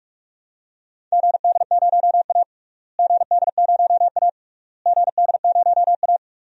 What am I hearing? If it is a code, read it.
Morse code, heard as a keyed tone, about 44 wpm, 690 Hz. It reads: GD0A GD0A GD0A
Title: DXCC-100 Call Signs - 4 Characters - Repeated 3x - 45wpm